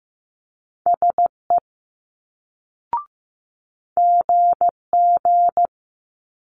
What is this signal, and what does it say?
Morse code, heard as a keyed tone, about 15 wpm, 700 Hz. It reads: SE GG
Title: Multi-letter Phonemes Quick Fire 15wpm